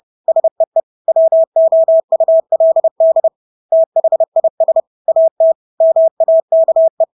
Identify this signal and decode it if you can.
Morse code, heard as a keyed tone, about 30 wpm, 650 Hz. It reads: SEE WOULD THIS AT MAKE
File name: Engelske ord 30-30 wpm